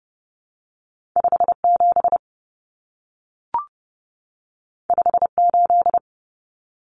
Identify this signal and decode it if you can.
Morse code, heard as a keyed tone, about 30 wpm, 700 Hz. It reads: 57 58
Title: Numbers - 2-Digit 30wpm